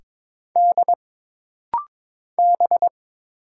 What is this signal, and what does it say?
Morse code, heard as a keyed tone, about 22 wpm, 700 Hz. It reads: D B
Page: This rapid-fire timing is geared to push your ICR (Instant Character Recognition) skills.